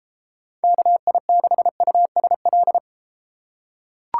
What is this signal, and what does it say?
Morse code, heard as a keyed tone, about 32 wpm, 700 Hz. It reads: KI6USL